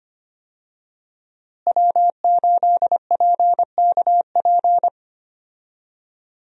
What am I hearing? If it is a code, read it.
Morse code, heard as a keyed tone, about 25 wpm, 700 Hz. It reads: W8PKP
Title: USA Call Signs - 5 Characters - Repeated 2x - 25wpm